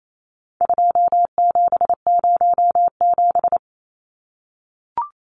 Morse code, heard as a keyed tone, about 28 wpm, 700 Hz. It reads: 2707
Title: Numbers - 4-Digit 28wpm